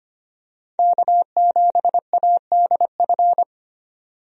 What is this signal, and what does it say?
Morse code, heard as a keyed tone, about 25 wpm, 700 Hz. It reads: K7ADF
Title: USA Call Signs - 5 Characters - 25wpm